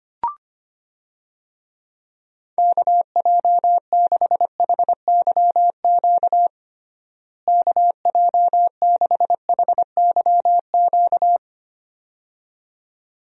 Morse code, heard as a keyed tone, about 25 wpm, 700 Hz. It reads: KJ6HYQ KJ6HYQ
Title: USA Call Signs - 6 Characters - Repeated 2x - 25wpm